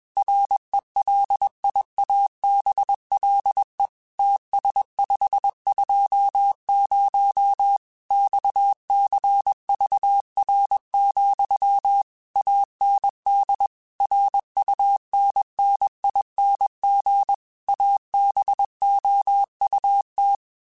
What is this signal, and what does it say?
Morse code, heard as a keyed tone, about 21 wpm, 800 Hz. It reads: RELIABLE TS520 XCVR, AND RUNNING ABOUT
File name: QSOs / QSOs-mp3s / WD0FBT 1@20